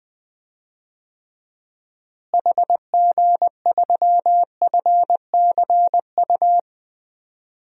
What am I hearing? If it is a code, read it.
Morse code, heard as a keyed tone, about 20 wpm, 700 Hz. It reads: HG3FCU
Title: DXCC-100 Call Signs - 6 Characters - Repeated 3x - 20wpm